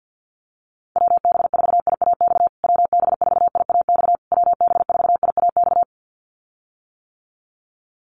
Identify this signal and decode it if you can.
Morse code, heard as a keyed tone, about 50 wpm, 700 Hz. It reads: WB4IAX WB4IAX WB4IAX
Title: USA Call Signs Repeated 3x 50wpm